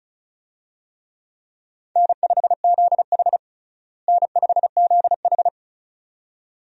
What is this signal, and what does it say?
Morse code, heard as a keyed tone, about 34 wpm, 700 Hz. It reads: N5ZH N5ZH
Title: USA Call Signs - 4 Characters - Repeated 2x - 35wpm